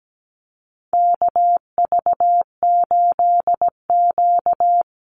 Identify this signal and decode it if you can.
Morse code, heard as a keyed tone, about 17 wpm, 700 Hz. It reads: KV8Q
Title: Call Signs - 17wpm